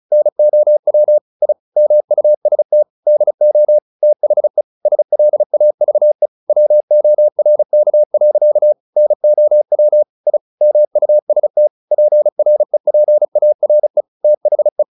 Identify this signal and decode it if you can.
Morse code, heard as a keyed tone, about 34 wpm, 600 Hz. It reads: NOW I MUST DO THE SLAVE WORK. NOW I MUST PREPARE THE
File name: the old man and the sea 2 35wpm